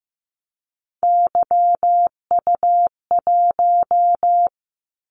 Morse code, heard as a keyed tone, about 15 wpm, 700 Hz. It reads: YU1
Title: Callsign Prefixes for Top 25 DXCC - 15wpm